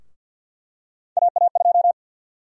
Morse code, heard as a keyed tone, about 50 wpm, 700 Hz. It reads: AA2
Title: USA Call Sign Prefixes - 50wpm